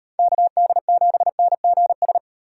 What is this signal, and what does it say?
Morse code, heard as a keyed tone, about 38 wpm, 700 Hz. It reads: KD7NGS